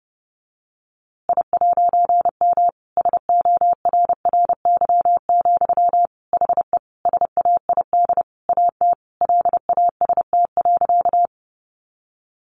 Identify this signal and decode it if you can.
Morse code, heard as a keyed tone, about 30 wpm, 700 Hz. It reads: I'M SORRY, HE SAID AT LAST.
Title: A Sound Of Thunder 30wpm